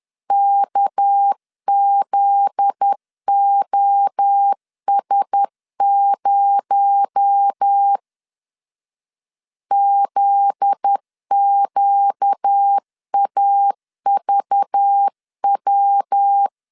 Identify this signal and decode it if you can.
Morse code, heard as a keyed tone, about 10 wpm, 800 Hz. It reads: KZOS0 ZQAVW
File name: LEZIONI / 7 GF9Z / 7 GF9Z 10wpm 2611